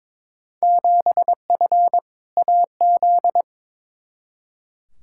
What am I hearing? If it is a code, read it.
Morse code, heard as a keyed tone, about 22 wpm, 700 Hz. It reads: 7F AZ
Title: Field Day Exchanges Encore - 22wpm